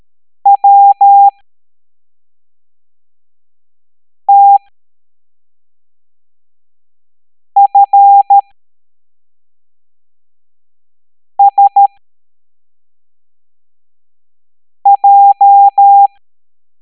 Morse code, heard as a keyed tone, about 13 wpm, spaced as for 3 wpm, 800 Hz. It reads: WTFSJ
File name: hamdata / L09-EX04